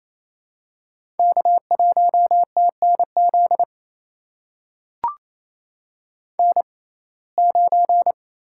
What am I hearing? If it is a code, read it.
Morse code, heard as a keyed tone, about 28 wpm, spaced as for 20 wpm, 700 Hz. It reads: K1TNZ N9
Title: USA Call Signs - 5 Characters - 6x Character Spacing - 28wpm